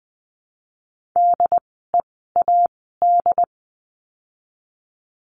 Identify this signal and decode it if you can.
Morse code, heard as a keyed tone, about 20 wpm, spaced as for 14 wpm, 700 Hz. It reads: DEAD
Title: 075 Lesson 13f - Letter L - Words 20wpm